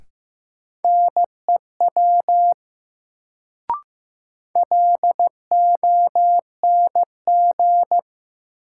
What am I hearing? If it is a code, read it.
Morse code, heard as a keyed tone, about 15 wpm, 700 Hz. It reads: NEW LONG
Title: Sets of 2 Words - New Word Drill - 15wpm